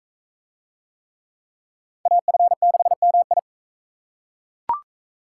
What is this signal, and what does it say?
Morse code, heard as a keyed tone, about 40 wpm, 700 Hz. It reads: AF6MI